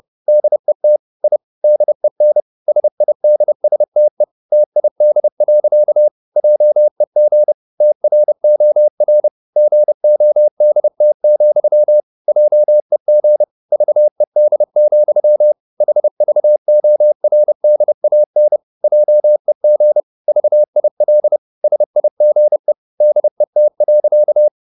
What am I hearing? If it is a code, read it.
Morse code, heard as a keyed tone, about 30 wpm, 600 Hz. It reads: DET I DEN SIDSTE TID. JEG TROR GODT, JEG VED, HVORDAN JEG VIL SIGE DET.
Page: Ferie 30 wpm